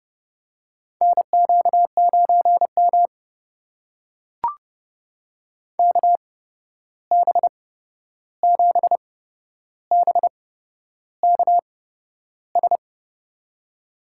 Morse code, heard as a keyed tone, about 30 wpm, spaced as for 11 wpm, 700 Hz. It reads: NQ9M KB7BKS
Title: USA Call Signs - 8x Character Spacing - 30wpm